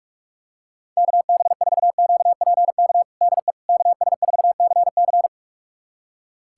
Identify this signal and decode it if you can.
Morse code, heard as a keyed tone, about 46 wpm, 700 Hz. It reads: KB4QPK DE KS4CC
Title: USA Callsign DE Callsign 45wpm